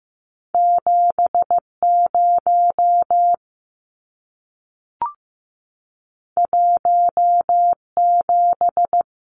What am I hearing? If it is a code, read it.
Morse code, heard as a keyed tone, about 15 wpm, 700 Hz. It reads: 70 17
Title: Numbers - 2-Digit 15wpm